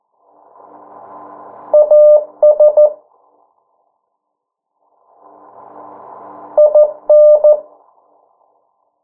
Morse code, heard as a keyed tone, about 14 wpm, 600 Hz. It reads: AS IN
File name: morse mp3 / 09 urok 8